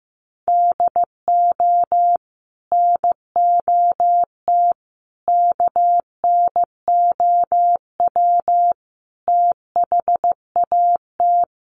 Morse code, heard as a keyed tone, about 15 wpm, 700 Hz. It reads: DO NOT KNOW THAT
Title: Sets of 5 Words Encore2 15wpm